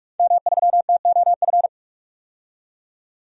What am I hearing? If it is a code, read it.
Morse code, heard as a keyed tone, about 42 wpm, 700 Hz. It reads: M3TOF
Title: DXCC-100 Call Signs - 5 Characters - 45wpm